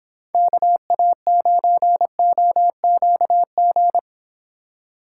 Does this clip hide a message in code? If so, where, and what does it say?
Yes: Morse code from 0.3 to 4.0 s, heard as a keyed tone, about 30 wpm, 700 Hz. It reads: KA9OQG